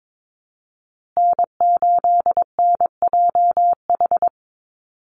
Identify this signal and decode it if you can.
Morse code, heard as a keyed tone, about 22 wpm, 700 Hz. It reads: N8NJH
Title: USA Call Signs Repeated 3x 22wpm